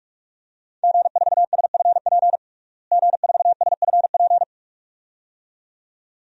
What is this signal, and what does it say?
Morse code, heard as a keyed tone, about 46 wpm, 700 Hz. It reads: G4SFP G4SFP
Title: DXCC-100 Call Signs - 5 Characters - Repeated 2x - 45wpm